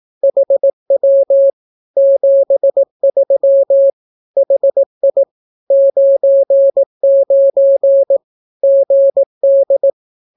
Morse code, heard as a keyed tone, about 18 wpm, 550 Hz. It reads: HW 73 HI 99 GD
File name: naukapodstaw / skroty / skroty